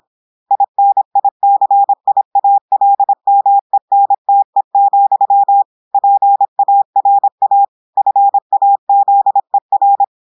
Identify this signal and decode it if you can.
Morse code, heard as a keyed tone, about 26 wpm, 800 Hz. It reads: INICIALMENTE, PARA FAZER
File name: portugais texte en clair 2 vitesse 26 mots minutes